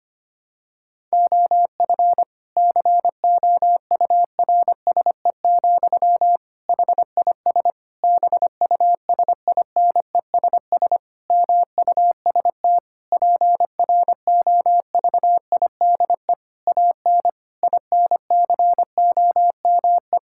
Morse code, heard as a keyed tone, about 25 wpm, 700 Hz. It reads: OF COURSE, HIS BUSINESS MUST PROVIDE AN INCOME
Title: Sentences from Top 1000 Words - Review Part 5 - 25wpm